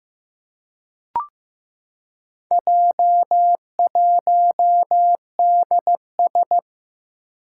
Morse code, heard as a keyed tone, about 15 wpm, 700 Hz. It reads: J1DS
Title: Four Challenging Letter-Number Combinations 15wpm